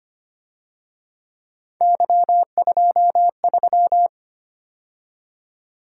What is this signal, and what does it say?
Morse code, heard as a keyed tone, about 29 wpm, 700 Hz. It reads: Y23
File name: Speed-Racing - Three Letter-Number Combinations - 25wpm